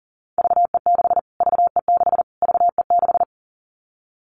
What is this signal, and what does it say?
Morse code, heard as a keyed tone, about 40 wpm, 700 Hz. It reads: VE6 VE6 VE6
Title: Callsign Prefixes for Top 25 DXCC - Repeated 3x - 40wpm